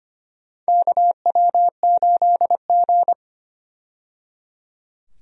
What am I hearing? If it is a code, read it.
Morse code, heard as a keyed tone, about 25 wpm, 700 Hz. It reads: KW8G
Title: USA Call Signs - 4 Characters - 25wpm